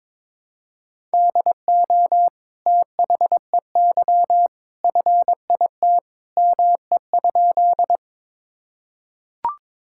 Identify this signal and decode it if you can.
Morse code, heard as a keyed tone, about 22 wpm, 700 Hz. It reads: DO THEY FIT ME?
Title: Sentences from Top 1000 Words - Review Part 5 - 22wpm